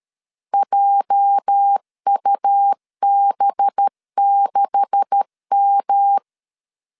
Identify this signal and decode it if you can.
Morse code, heard as a keyed tone, about 13 wpm, 800 Hz. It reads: JUB6M
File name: CW / DET / 1 9 1500C S B 12wpm 3819